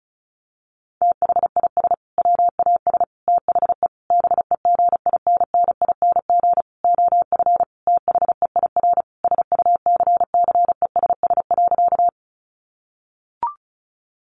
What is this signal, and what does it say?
Morse code, heard as a keyed tone, about 34 wpm, 700 Hz. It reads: THIS WAS THE BEGINNING OF THEIR SUCCESS.
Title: Sentences from Top 900 Words 35wpm